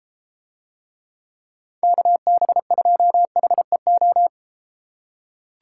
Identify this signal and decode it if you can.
Morse code, heard as a keyed tone, about 33 wpm, 700 Hz. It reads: KB2HEO